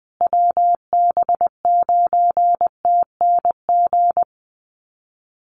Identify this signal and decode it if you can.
Morse code, heard as a keyed tone, about 20 wpm, 700 Hz. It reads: WB9TNG
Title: USA Call Signs Repeated 3x 20wpm